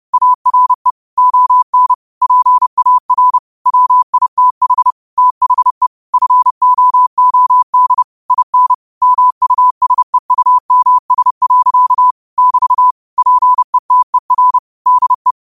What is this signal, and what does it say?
Morse code, heard as a keyed tone, about 30 wpm, 1000 Hz. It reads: ARE ON PAR WITH THE FOOD IN MUSEUMS. <BT> PETER DE